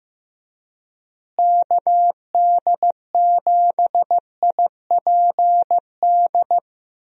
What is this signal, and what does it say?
Morse code, heard as a keyed tone, about 16 wpm, 700 Hz. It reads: KD7IPD
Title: USA Call Signs - 6 Characters - Repeated 2x - 15wpm